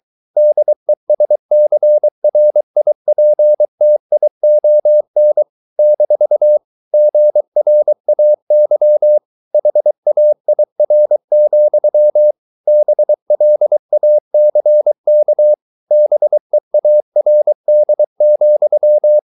Morse code, heard as a keyed tone, about 23 wpm, 600 Hz. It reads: DESCRIPTION - GRAY HAIR, BLACK BEARD,